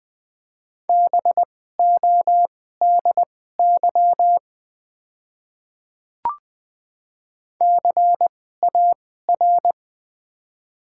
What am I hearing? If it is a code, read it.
Morse code, heard as a keyed tone, about 20 wpm, spaced as for 14 wpm, 700 Hz. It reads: BODY CAR
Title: 2x Character Spacing - Top 500 Words - 20wpm